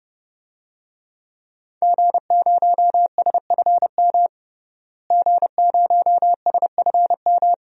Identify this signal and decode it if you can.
Morse code, heard as a keyed tone, about 32 wpm, 700 Hz. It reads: G0SFM G0SFM
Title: DXCC-100 Call Signs - 5 Characters - Repeated 2x - 30wpm